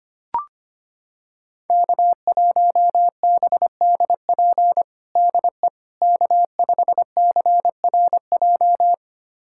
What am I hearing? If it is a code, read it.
Morse code, heard as a keyed tone, about 25 wpm, 700 Hz. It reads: K1BDP DE K5CRJ
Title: USA Callsign DE Callsign - 5 Characters - 25wpm